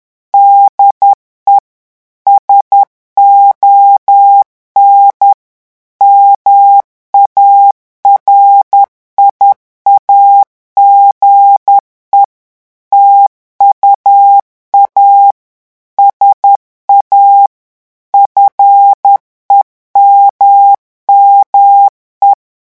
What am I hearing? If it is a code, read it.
Morse code, heard as a keyed tone, about 10 wpm, 800 Hz. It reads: DE SON MARIAGE TUA SA FEMME
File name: revision9 citations humoristiques vitesse 10 mots minute